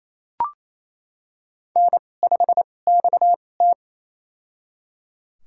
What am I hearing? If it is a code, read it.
Morse code, heard as a keyed tone, about 28 wpm, spaced as for 20 wpm, 700 Hz. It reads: N5XT
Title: USA Call Signs - 4 Characters - 2x Character Spacing - 28wpm